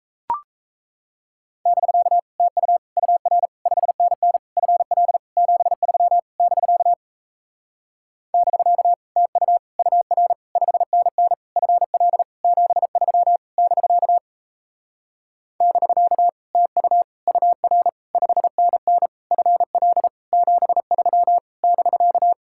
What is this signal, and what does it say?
Morse code, heard as a keyed tone, about 37 wpm, 700 Hz. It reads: <BK> TU UR 5NN FL 73 <BK> <BK> TU UR 5NN FL 73 <BK> <BK> TU UR 5NN FL 73 <BK>